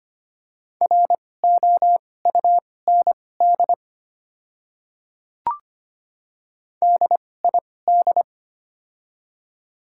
Morse code, heard as a keyed tone, about 25 wpm, spaced as for 18 wpm, 690 Hz. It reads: ROUND DID
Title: 2x Character Spacing - Top 300 Words - 25wpm